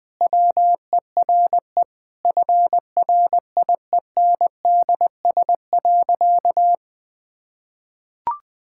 Morse code, heard as a keyed tone, about 20 wpm, 700 Hz. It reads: WERE FRIENDS.
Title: Sentences with 2-Word Phrases - 20wpm